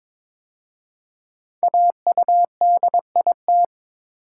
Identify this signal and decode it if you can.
Morse code, heard as a keyed tone, about 22 wpm, 700 Hz. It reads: AUDIT
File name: 5 Letter Words - First 3 Letters Given - 22wpm